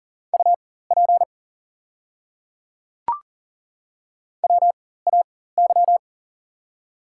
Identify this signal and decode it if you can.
Morse code, heard as a keyed tone, about 40 wpm, spaced as for 19 wpm, 700 Hz. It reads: UP WAY
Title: 4x Character Spacing - Top 300 Words - 40wpm